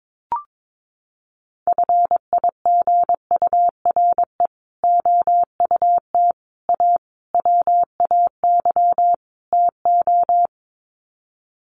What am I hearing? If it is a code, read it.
Morse code, heard as a keyed tone, about 22 wpm, 700 Hz. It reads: FIGURE OUT A WAY TO
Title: Sets of 5 Words Encore2 22wpm